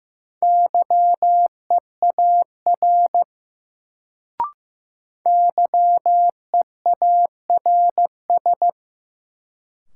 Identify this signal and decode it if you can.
Morse code, heard as a keyed tone, about 15 wpm, 700 Hz. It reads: YEAR YEARS
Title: Sentences from Top 200 Words - New Word Drill - 15wpm